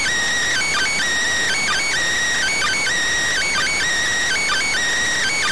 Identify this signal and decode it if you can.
Emergency Alert System (SAME) tones, heard as harsh buzzing bursts. It is heard from 0.5 to 4.7 s.